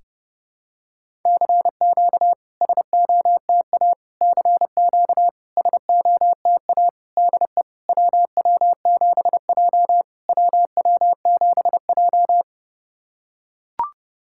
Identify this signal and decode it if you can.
Morse code, heard as a keyed tone, about 30 wpm, 700 Hz. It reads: CQ SOTA CQ SOTA DE WW7J WW7J
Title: SOTA QSO Practice 30wpm